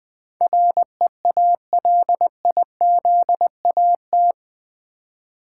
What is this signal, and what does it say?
Morse code, heard as a keyed tone, about 20 wpm, 700 Hz. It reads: REALIZAT
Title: Build A Word - 4 Letters At A Time - 11-Letter Words - 20wpm